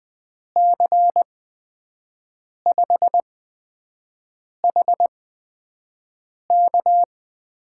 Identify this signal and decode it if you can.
Morse code, heard as a keyed tone, about 20 wpm, spaced as for 5 wpm, 700 Hz. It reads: C5HK
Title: USA Call Signs - 6 Characters - 8x Character Spacing - 20wpm